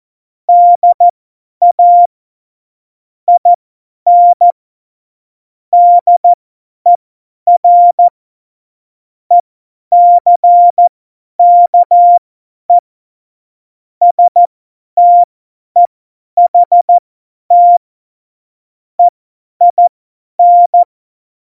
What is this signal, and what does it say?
Morse code, heard as a keyed tone, about 14 wpm, spaced as for 10 wpm, 700 Hz. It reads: DA IN DER ECKE STEHT EIN
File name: Morsekurs / Witze / Witz 28. November 2017, 50 F70